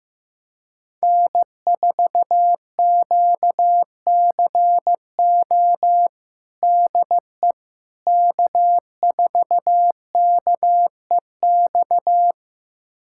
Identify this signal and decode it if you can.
Morse code, heard as a keyed tone, about 15 wpm, 700 Hz. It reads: N4QCO DE K4KEX
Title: USA Callsign DE Callsign 15wpm